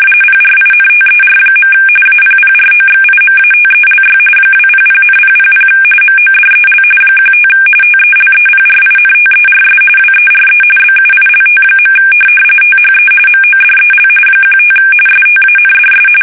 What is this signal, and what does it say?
RTTY, about 45.45 baud. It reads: PRINTER. APPROXIMATELY 200,000 MODEL 15 TELEPRINTERS WERE BUILT. THE MODEL 15 STANDS
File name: fsk850 / teletype-corp